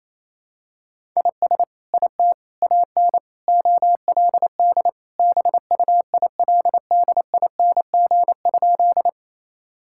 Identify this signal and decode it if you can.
Morse code, heard as a keyed tone, about 28 wpm, 700 Hz. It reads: IS IT AN OLD BUILDING?
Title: Sentences from Top 1000 Words - Review Part 5 - 28wpm